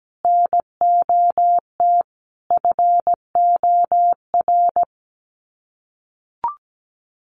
Morse code, heard as a keyed tone, about 17 wpm, 700 Hz. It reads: NOT FOR
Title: Sets of 5 Words Encore 17wpm